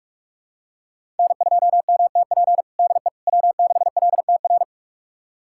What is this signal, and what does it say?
Morse code, heard as a keyed tone, about 41 wpm, 700 Hz. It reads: N2MTP DE W6LTR
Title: USA Callsign DE Callsign - 5 Characters - 45wpm